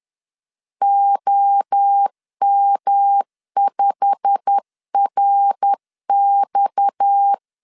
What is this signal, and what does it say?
Morse code, heard as a keyed tone, about 10 wpm, 800 Hz. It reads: OM5RX
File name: LEZIONI / 8 YJ45 / 8 YJ45 10wpm 2549